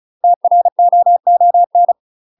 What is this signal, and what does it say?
Morse code, heard as a keyed tone, about 36 wpm, 700 Hz. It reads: TROON
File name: UK MASTER QSOs TRAINING / 35WPM / UK MASTER QSOs 0010